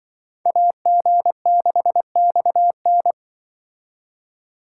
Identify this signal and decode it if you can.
Morse code, heard as a keyed tone, about 23 wpm, 700 Hz. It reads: AG6XN
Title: Speed Racing - USA 2x2 Call Signs 20wpm